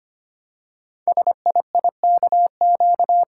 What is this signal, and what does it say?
Morse code, heard as a keyed tone, about 25 wpm, 700 Hz. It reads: SIIKQ
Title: Code Groups - 5 Letters - 25wpm